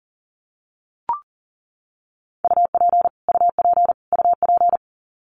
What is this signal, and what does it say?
Morse code, heard as a keyed tone, about 40 wpm, 700 Hz. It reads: UP UP UP
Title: Top 200 Words Repeated 40wpm